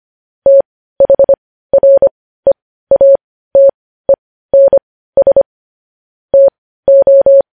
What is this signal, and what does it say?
Morse code, heard as a keyed tone, about 25 wpm, spaced as for 15 wpm, 550 Hz. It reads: THREATENS TO
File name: podcast / News headlines-1x-25x15